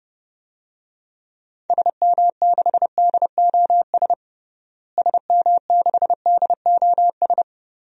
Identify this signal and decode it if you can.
Morse code, heard as a keyed tone, about 30 wpm, 700 Hz. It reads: SM6DOS SM6DOS
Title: DXCC-100 Call Signs - 6 Characters - Repeated 2x - 30wpm